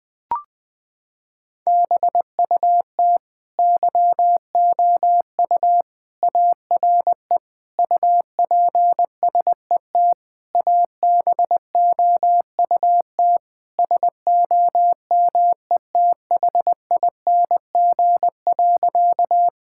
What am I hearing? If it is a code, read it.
Morse code, heard as a keyed tone, about 20 wpm, 700 Hz. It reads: BUT YOU ARE UPSET ABOUT SOMETHING.
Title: Sentences from Top 1000 Words - Review Part 5 - 20wpm